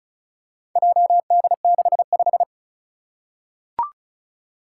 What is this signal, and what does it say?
Morse code, heard as a keyed tone, about 36 wpm, 690 Hz. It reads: JD65
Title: Four Challenging Letter-Number Combinations 35wpm